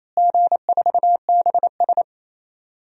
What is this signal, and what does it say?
Morse code, heard as a keyed tone, about 28 wpm, 700 Hz. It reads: G4BS
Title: DXCC-100 Call Signs - 4 Characters - Repeated 2x - 28wpm